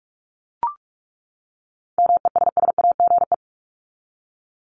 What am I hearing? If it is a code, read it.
Morse code, heard as a keyed tone, about 44 wpm, 700 Hz. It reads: MESSAGE
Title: Top 1001-1250 Words 45wpm